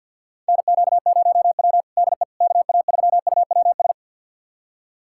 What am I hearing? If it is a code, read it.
Morse code, heard as a keyed tone, about 50 wpm, 700 Hz. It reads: NQ0W DE KA3UWS
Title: USA Callsign DE Callsign 50wpm